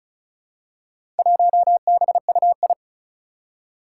Morse code, heard as a keyed tone, about 35 wpm, 700 Hz. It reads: 1BUI
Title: Four Challenging Letter-Number Combinations 35wpm